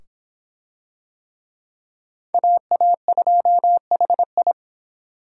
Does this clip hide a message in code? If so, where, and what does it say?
Yes: Morse code from 2.3 to 4.5 s, heard as a keyed tone, about 24 wpm, 700 Hz. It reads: AA2HI